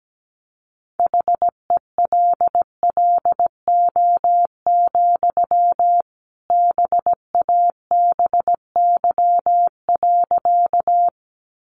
Morse code, heard as a keyed tone, about 17 wpm, 700 Hz. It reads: HELLO, BABY.
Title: Sentences from Top 1300 Words 17wpm